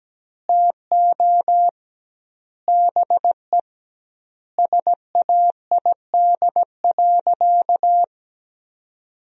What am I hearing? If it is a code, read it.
Morse code, heard as a keyed tone, about 17 wpm, 700 Hz. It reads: TO BE SAID.
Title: 2x Word-Spacing - Sentences from Top 500 Words - 17wpm